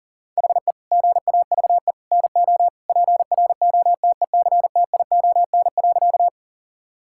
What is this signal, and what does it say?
Morse code, heard as a keyed tone, about 37 wpm, 700 Hz. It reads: HE GAVE NO PROTECTION.
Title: Sentences from Top 1600 Words 40wpm